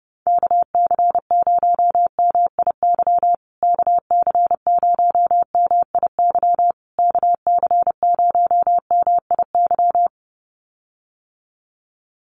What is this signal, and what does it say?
Morse code, heard as a keyed tone, about 28 wpm, 700 Hz. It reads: KC0MIY KC0MIY KC0MIY
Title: USA Call Signs Repeated 3x 30wpm